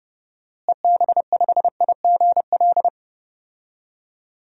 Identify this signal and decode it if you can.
Morse code, heard as a keyed tone, about 31 wpm, 700 Hz. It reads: EB5IGL
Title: DXCC-100 Call Signs - 6 Characters - Repeated 3x - 30wpm